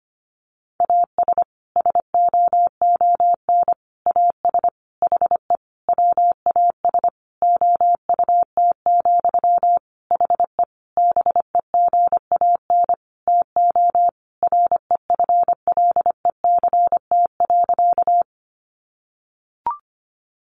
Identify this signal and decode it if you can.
Morse code, heard as a keyed tone, about 25 wpm, 700 Hz. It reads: AS SOON AS HE WAS OUT, HE BEGAN TO REFLECT.
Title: Sentences from Top 1500 Words 25wpm